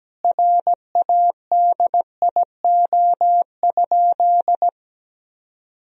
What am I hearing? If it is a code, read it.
Morse code, heard as a keyed tone, about 17 wpm, 700 Hz. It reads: RADIO?
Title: Sentences from Top 1000 Words - Review Part 3 - 17wpm